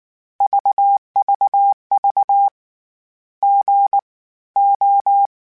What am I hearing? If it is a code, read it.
Morse code, heard as a keyed tone, about 19 wpm, spaced as for 15 wpm, 800 Hz. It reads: VVV GO